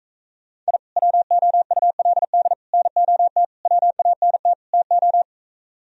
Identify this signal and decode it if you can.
Morse code, heard as a keyed tone, about 45 wpm, 700 Hz. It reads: I WOULD NOT WANT TO